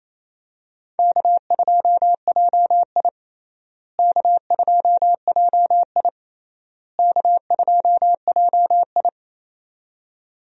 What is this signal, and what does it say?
Morse code, heard as a keyed tone, about 28 wpm, 700 Hz. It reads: K2JI K2JI K2JI
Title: USA Call Signs - 4 Characters - Repeated 3x - 28wpm